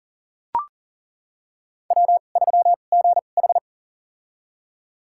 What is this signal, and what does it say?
Morse code, heard as a keyed tone, about 40 wpm, spaced as for 29 wpm, 700 Hz. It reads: W3GH
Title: USA Call Signs - 4 Characters - 2x Character Spacing - 40wpm